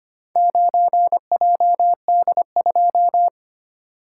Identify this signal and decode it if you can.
Morse code, heard as a keyed tone, about 25 wpm, 690 Hz. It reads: 9JD2
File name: practice / 25wpm / Four Letter-Number Combinations - 25wpm